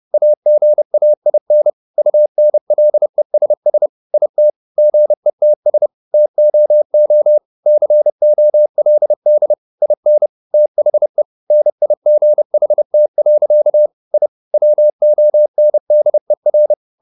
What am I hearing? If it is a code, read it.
Morse code, heard as a keyed tone, about 30 wpm, 600 Hz. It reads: AGAIN UNLESS IT GETS TOO COLD IN THE NIGHT. I WONDER
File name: the old man and the sea 2 30wpm